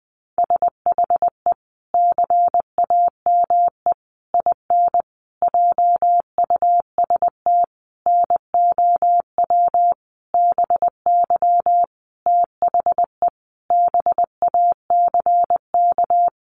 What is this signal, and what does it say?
Morse code, heard as a keyed tone, about 20 wpm, 700 Hz. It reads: SHE CAME IN JUST NOW BY THE BACK
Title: Sentences from Top 300 Words 20wpm